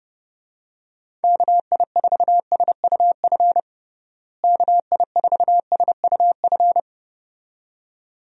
Morse code, heard as a keyed tone, about 30 wpm, 700 Hz. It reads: KI4SUF KI4SUF
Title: USA Call Signs - 6 Characters - Repeated 2x - 30wpm